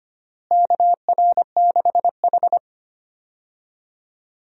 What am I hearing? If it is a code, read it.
Morse code, heard as a keyed tone, about 25 wpm, 700 Hz. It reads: KR6H
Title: USA Call Signs - 4 Characters - Repeated 2x - 25wpm